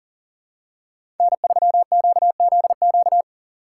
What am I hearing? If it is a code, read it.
Morse code, heard as a keyed tone, about 40 wpm, 700 Hz. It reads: N3QZQ
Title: USA Call Signs - 4x Character Spacing - 40wpm